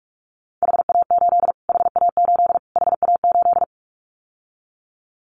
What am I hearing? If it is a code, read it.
Morse code, heard as a keyed tone, about 44 wpm, 700 Hz. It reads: HA8 HA8 HA8
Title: Callsign Prefixes for Top 25 DXCC - Repeated 3x - 45wpm